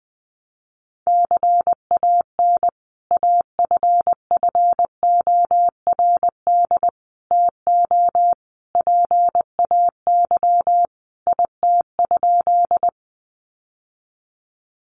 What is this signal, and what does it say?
Morse code, heard as a keyed tone, about 20 wpm, 700 Hz. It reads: CAN AFFORD TO PAY IT?
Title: Sentences from Top 1100 Words 20wpm